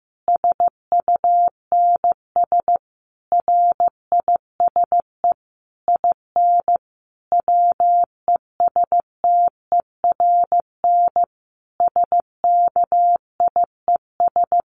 Morse code, heard as a keyed tone, about 15 wpm, 700 Hz. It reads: SUNS RISE IN WESTERN SKIES
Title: A Sound Of Thunder 15wpm